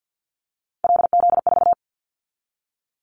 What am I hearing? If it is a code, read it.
Morse code, heard as a keyed tone, about 50 wpm, 700 Hz. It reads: LZ4
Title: Callsign Prefixes for Top 25 DXCC - 50wpm